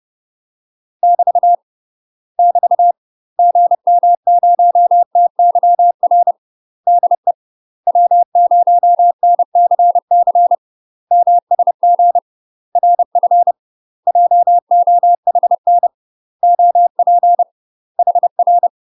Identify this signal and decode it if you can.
Morse code, heard as a keyed tone, about 30 wpm, 700 Hz. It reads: <BT> <BT> GM0TYR DE W0NCC MSG RF JOHN OP HR
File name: UK MASTER QSOs TRAINING / 30WPM / UK MASTER QSOs 0042